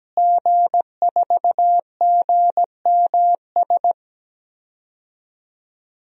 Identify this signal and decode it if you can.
Morse code, heard as a keyed tone, about 17 wpm, 700 Hz. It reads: G4GMS
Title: DXCC-100 Call Signs - 5 Characters - Repeated 3x - 17wpm